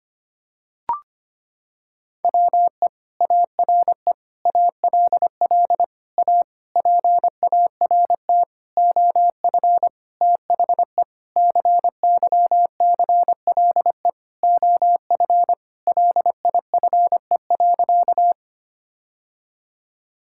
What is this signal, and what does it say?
Morse code, heard as a keyed tone, about 25 wpm, 700 Hz. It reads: WE ARE ALL A PART OF THE CYCLE OF LIFE.
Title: Sentences from Top 1000 Words - Review Part 6 - 25wpm